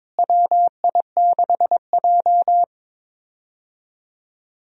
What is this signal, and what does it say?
Morse code, heard as a keyed tone, about 22 wpm, 700 Hz. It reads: WI6J
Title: USA Call Signs - 4 Characters - Repeated 2x - 22wpm